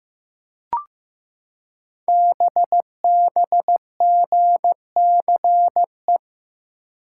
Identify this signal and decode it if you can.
Morse code, heard as a keyed tone, about 15 wpm, 700 Hz. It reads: BBGCE
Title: Code Groups - 5 Letter:Numbers - 15wpm